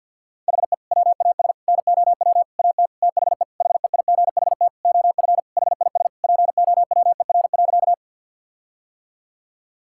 Morse code, heard as a keyed tone, about 52 wpm, 700 Hz. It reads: HE WAS NOW AT THE HEIGHT OF HIS POWER.
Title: Sentences from Top 1600 Words 50wpm